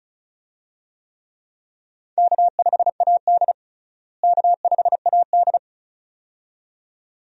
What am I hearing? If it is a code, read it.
Morse code, heard as a keyed tone, about 36 wpm, 700 Hz. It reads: K5AD K5AD
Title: USA Call Signs - 4 Characters - Repeated 2x - 35wpm